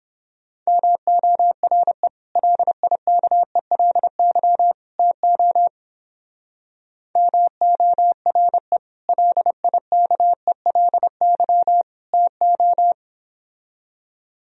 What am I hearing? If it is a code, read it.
Morse code, heard as a keyed tone, about 31 wpm, 700 Hz. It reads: MORE LIKELY TO MORE LIKELY TO